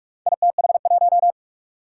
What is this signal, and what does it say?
Morse code, heard as a keyed tone, about 44 wpm, 700 Hz. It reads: ITH1
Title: Four Challenging Letter-Number Combinations 45wpm